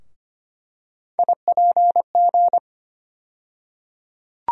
Morse code, heard as a keyed tone, about 25 wpm, 700 Hz. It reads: IPG
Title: USA Call Sign Postfixes - 3 Letters 25wpm